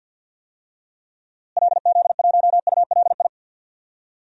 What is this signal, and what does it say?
Morse code, heard as a keyed tone, about 52 wpm, 700 Hz. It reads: LZ1ULI
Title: DXCC-100 Call Signs - 6 Characters - Repeated 2x - 50wpm